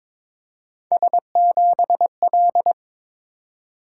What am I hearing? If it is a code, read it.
Morse code, heard as a keyed tone, about 22 wpm, 700 Hz. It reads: S7L
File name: practice / 22wpm / Three Letter-Number Combinations - 22wpm